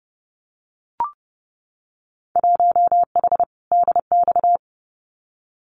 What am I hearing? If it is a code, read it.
Morse code, heard as a keyed tone, about 30 wpm, 700 Hz. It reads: 1H DX
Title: Winter Field Day Exchanges 30wpm